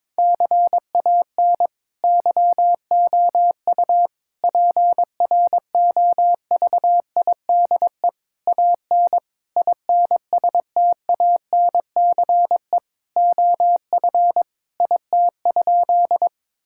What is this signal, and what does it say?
Morse code, heard as a keyed tone, about 22 wpm, 700 Hz. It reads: CAN YOU PROVIDE AN INSTANCE OF IT?
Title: Sentences from Top 1000 Words - Review Part 5 - 22wpm